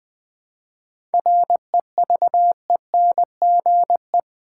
Morse code, heard as a keyed tone, about 20 wpm, 700 Hz. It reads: REVENGE
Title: 7 Letter Words - First 3 Letters Given - 20wpm